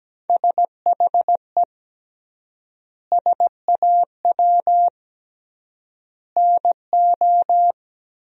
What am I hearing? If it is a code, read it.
Morse code, heard as a keyed tone, about 17 wpm, 700 Hz. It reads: SHE SAW NO
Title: 3x Word-Spacing - Sentences from Top 300 Words - 17wpm